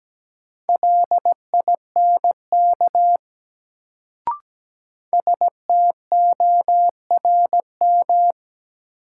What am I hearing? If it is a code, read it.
Morse code, heard as a keyed tone, about 17 wpm, 700 Hz. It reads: LINK STORM
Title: Sentences from Top 1500 Words - New Word Drill Alt - 17wpm